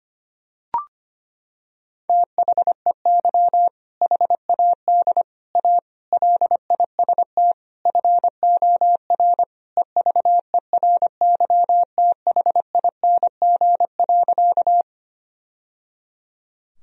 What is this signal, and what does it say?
Morse code, heard as a keyed tone, about 25 wpm, 700 Hz. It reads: THEY HAD A LIST FOR EVERYTHING.
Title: Sentences from Top 1000 Words - Review Part 2 - 25wpm